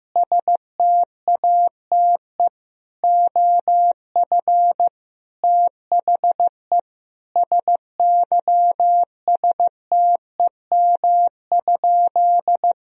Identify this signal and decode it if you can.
Morse code, heard as a keyed tone, about 15 wpm, 700 Hz. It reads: STATE OF THE SYSTEM?
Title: Sentences from Top 1000 Words - Review Part 3 - 15wpm